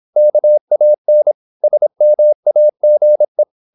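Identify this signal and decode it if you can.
Morse code, heard as a keyed tone, about 26 wpm, 600 Hz. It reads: KAN SMAGE
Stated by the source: High-speed (QRQ) Morse at 26 wpm.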